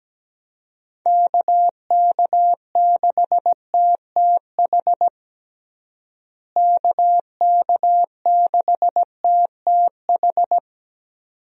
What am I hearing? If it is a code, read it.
Morse code, heard as a keyed tone, about 17 wpm, 700 Hz. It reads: KK6TTH KK6TTH
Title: USA Call Signs - 6 Characters - Repeated 2x - 17wpm